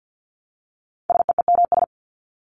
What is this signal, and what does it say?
Morse code, heard as a keyed tone, about 48 wpm, 700 Hz. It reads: SEEMS
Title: ES:S Suffix - Top 100 Words with Base Words 50wpm